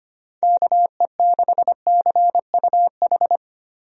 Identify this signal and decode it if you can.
Morse code, heard as a keyed tone, about 25 wpm, 700 Hz. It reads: KE6CUH
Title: USA Call Signs - 6 Characters - Repeated 3x - 25wpm